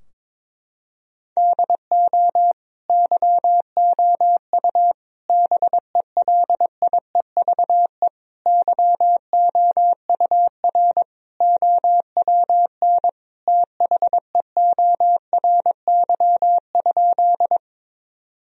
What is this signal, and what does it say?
Morse code, heard as a keyed tone, about 22 wpm, 700 Hz. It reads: DO YOU BELIEVE YOUR OWN THEORY?
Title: Sentences from Top 1000 Words - Review Part 3 - 22wpm